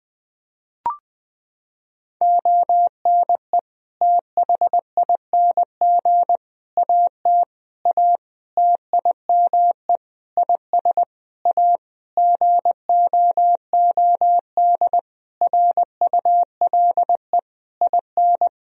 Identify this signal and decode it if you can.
Morse code, heard as a keyed tone, about 21 wpm, 700 Hz. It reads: ONE THING AT A TIME IS A GOOD RULE IN
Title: Sentences from Top 1000 Words - Review Part 2 - 20wpm